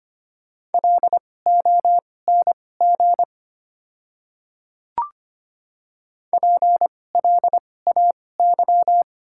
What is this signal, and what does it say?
Morse code, heard as a keyed tone, about 25 wpm, spaced as for 18 wpm, 690 Hz. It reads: LONG PLAY
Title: 2x Character Spacing - Top 200 Words - 25wpm